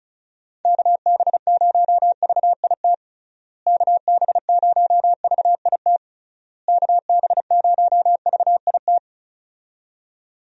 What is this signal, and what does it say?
Morse code, heard as a keyed tone, about 36 wpm, 700 Hz. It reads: KB0VIT KB0VIT KB0VIT
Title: USA Call Signs - 6 Characters - Repeated 3x - 35wpm